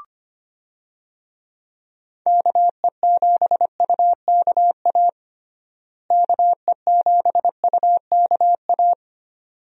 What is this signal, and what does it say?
Morse code, heard as a keyed tone, about 25 wpm, 700 Hz. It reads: KE7UKA KE7UKA
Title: USA Call Signs - 6 Characters - Repeated 2x - 25wpm